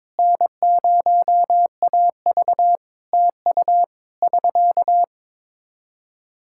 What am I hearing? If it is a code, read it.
Morse code, heard as a keyed tone, about 22 wpm, 700 Hz. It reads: N0AV TU <SK>
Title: Practice QSO - Exchange 4 - Ending - 22wpm